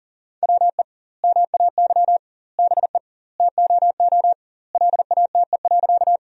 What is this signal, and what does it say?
Morse code, heard as a keyed tone, about 40 wpm, 700 Hz. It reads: WE MAY BE TOO LATE.
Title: 2x Word-Spacing - Sentences from Top 300 Words - 40wpm